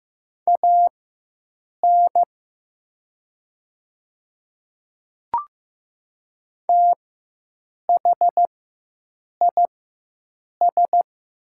Morse code, heard as a keyed tone, about 16 wpm, spaced as for 7 wpm, 700 Hz. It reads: AN THIS
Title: 4x Character Spacing - Top 100 Words - 15wpm